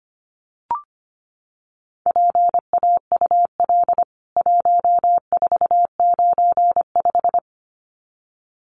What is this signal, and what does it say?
Morse code, heard as a keyed tone, about 25 wpm, 700 Hz. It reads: PAUL 1495
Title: CWT Exchange 25wpm